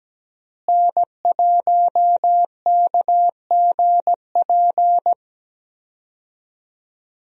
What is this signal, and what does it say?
Morse code, heard as a keyed tone, about 17 wpm, 700 Hz. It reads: N1KGP
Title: USA Call Signs - 5 Characters - Repeated 3x - 17wpm